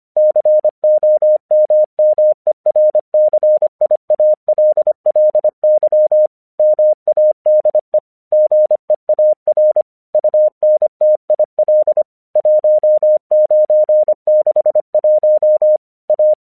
Morse code, heard as a keyed tone, about 25 wpm, 610 Hz. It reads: COMMERCIALLY MADE GEAR UNTIL 1961 A
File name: 1 English 25wpm / AQC12